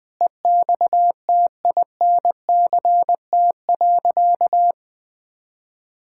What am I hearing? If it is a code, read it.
Morse code, heard as a keyed tone, about 20 wpm, 700 Hz. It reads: EXTINCT.
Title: Pangram Challenge v2 - 20wpm